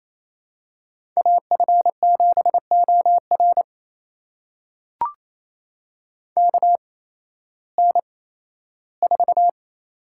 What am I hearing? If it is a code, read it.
Morse code, heard as a keyed tone, about 28 wpm, spaced as for 15 wpm, 700 Hz. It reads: AF7OR KN4
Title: USA Call Signs - 5 Characters - 8x Character Spacing - 28wpm